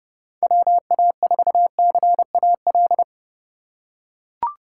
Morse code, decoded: WA4CAL

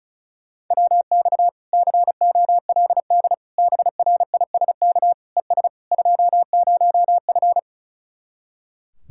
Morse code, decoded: WX COLD BRISK ES 20F

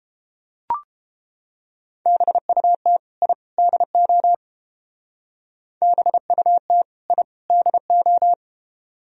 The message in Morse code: BUT I DO BUT I DO

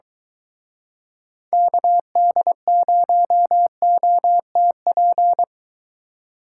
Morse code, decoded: KD0OTP

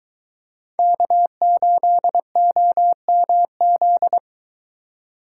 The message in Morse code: K8OMZ

